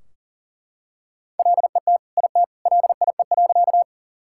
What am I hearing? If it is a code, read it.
Morse code, heard as a keyed tone, about 40 wpm, 700 Hz. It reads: LET IT LIE.
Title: Sentences from Top 1600 Words 40wpm